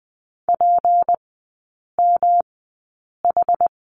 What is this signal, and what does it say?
Morse code, heard as a keyed tone, about 20 wpm, spaced as for 8 wpm, 700 Hz. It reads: PMH
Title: 198 Lesson 32e - Letter k - Three Characters 20wpm